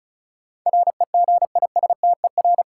Morse code, heard as a keyed tone, about 36 wpm, 700 Hz. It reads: REGISTER
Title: Sentences from Top 1200 Words - New Word Drill Alt - 35wpm